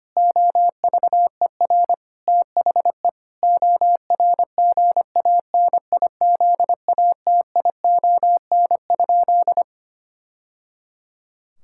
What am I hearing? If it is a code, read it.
Morse code, heard as a keyed tone, about 25 wpm, 700 Hz. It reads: OVER THE ORGANIZATION?
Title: Sentences from Top 1000 Words - Review Part 5 - 25wpm